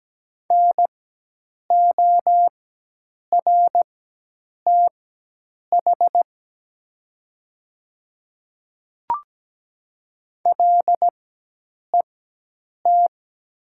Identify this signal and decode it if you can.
Morse code, heard as a keyed tone, about 17 wpm, spaced as for 8 wpm, 700 Hz. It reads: NORTH LET
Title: 4x Character Spacing - Top 300 Words - 17wpm